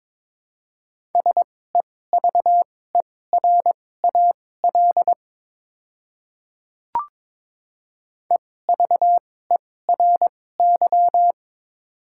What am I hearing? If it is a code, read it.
Morse code, heard as a keyed tone, about 21 wpm, spaced as for 16 wpm, 700 Hz. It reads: SEVERAL EVERY
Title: 2x Character Spacing - Top 500 Words - 22wpm